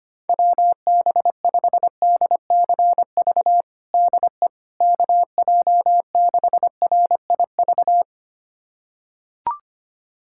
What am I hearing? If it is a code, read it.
Morse code, heard as a keyed tone, about 25 wpm, 700 Hz. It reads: WB5DCV DE KJ6RIV
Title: USA Callsign DE Callsign - 6 Characters - 25wpm